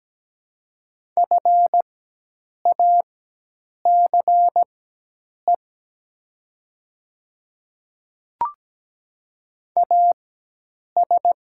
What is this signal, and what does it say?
Morse code, heard as a keyed tone, about 17 wpm, spaced as for 8 wpm, 700 Hz. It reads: FACE AS
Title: 4x Character Spacing - Top 300 Words - 17wpm